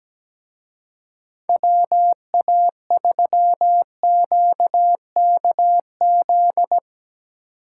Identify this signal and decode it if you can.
Morse code, heard as a keyed tone, about 19 wpm, 700 Hz. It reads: WA3QKZ
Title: Speed Racing - USA 2x3 Call Signs 17wpm